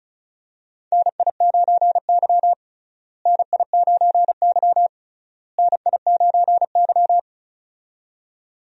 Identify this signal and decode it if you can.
Morse code, heard as a keyed tone, about 34 wpm, 700 Hz. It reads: NI9Y NI9Y NI9Y
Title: USA Call Signs - 4 Characters - Repeated 3x - 35wpm